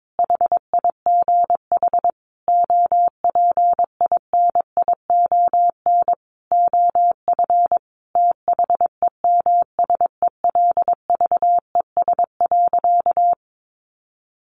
Morse code, heard as a keyed tone, about 22 wpm, 700 Hz. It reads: HIGH OPINION OF THEMSELVES.
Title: Sentences from Top 1300 Words 22wpm